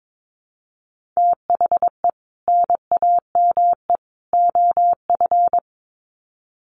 Morse code, heard as a keyed tone, about 22 wpm, 700 Hz. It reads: THE NAME OF
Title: Sets of 3 Words 22wpm